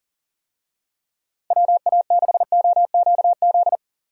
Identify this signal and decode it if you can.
Morse code, heard as a keyed tone, about 40 wpm, 700 Hz. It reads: WA6OQZ
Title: USA Call Signs - 6 Characters - 40wpm